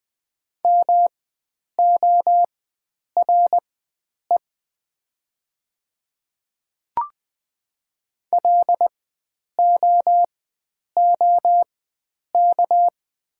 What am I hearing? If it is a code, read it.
Morse code, heard as a keyed tone, about 20 wpm, spaced as for 9 wpm, 700 Hz. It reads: MORE LOOK
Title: 4x Character Spacing - Top 100 Words - 20wpm